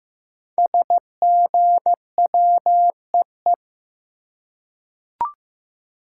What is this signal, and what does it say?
Morse code, heard as a keyed tone, about 15 wpm, 700 Hz. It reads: SGWEE
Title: Code Groups - 5 Letters - 15wpm